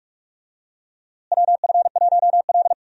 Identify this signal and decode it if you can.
Morse code, heard as a keyed tone, about 47 wpm, 700 Hz. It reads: WF1L
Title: USA Call Signs - 4 Characters - Repeated 3x - 45wpm